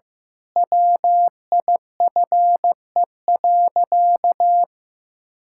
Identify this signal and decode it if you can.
Morse code, heard as a keyed tone, about 15 wpm, 700 Hz. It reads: WIFE.
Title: Pangram Challenge v2 - 15wpm